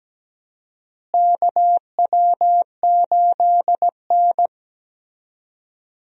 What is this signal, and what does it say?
Morse code, heard as a keyed tone, about 17 wpm, 700 Hz. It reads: KW8N
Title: USA Call Signs - 4 Characters - Repeated 2x - 17wpm